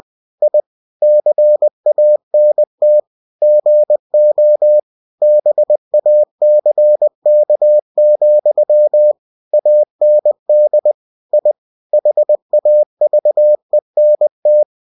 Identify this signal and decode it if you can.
Morse code, heard as a keyed tone, about 20 wpm, 600 Hz. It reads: I CANT GO BACK, AND I HAVENT
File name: cw-books / winnie-the-pooh / winnie-20wpm-page-0053